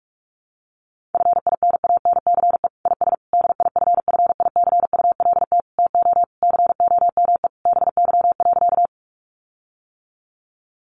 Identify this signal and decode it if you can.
Morse code, heard as a keyed tone, about 44 wpm, 700 Hz. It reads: FINANCE IS DIFFICULT TO COME BY.
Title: Sentences from Top 1500 Words 45wpm